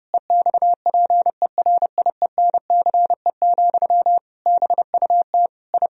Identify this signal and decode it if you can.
Morse code, heard as a keyed tone, about 30 wpm, 700 Hz. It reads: EXPERIENCE, BUT I
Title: Sentences from Top 1000 Words - Review Part 4 - 30wpm